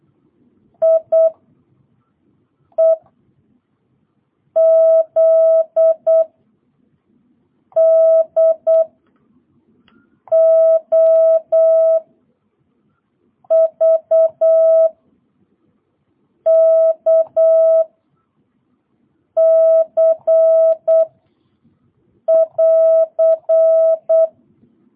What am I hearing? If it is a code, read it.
Morse code, heard as a keyed tone, about 8 wpm, 660 Hz. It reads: I E Z D O V K C <AR>